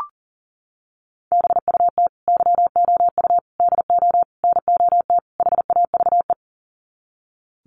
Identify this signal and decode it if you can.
Morse code, heard as a keyed tone, about 37 wpm, 700 Hz. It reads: BUT YOU DO NOT HAVE
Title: Sets of 5 Words Encore 40wpm